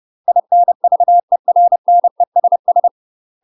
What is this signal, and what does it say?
Morse code, heard as a keyed tone, about 30 wpm, 700 Hz. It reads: INVERNESS
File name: UK MASTER QSOs TRAINING / 30WPM / UK MASTER QSOs 0041